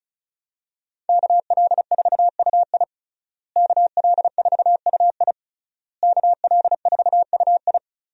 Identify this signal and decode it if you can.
Morse code, heard as a keyed tone, about 36 wpm, 700 Hz. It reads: KL4UI KL4UI KL4UI
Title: USA Call Signs - 5 Characters - Repeated 3x - 35wpm